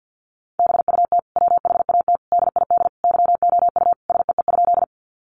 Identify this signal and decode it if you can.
Morse code, heard as a keyed tone, about 50 wpm, 700 Hz. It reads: BUT WHAT DID YOU SEE?
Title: Sentences from Top 100 Words 50wpm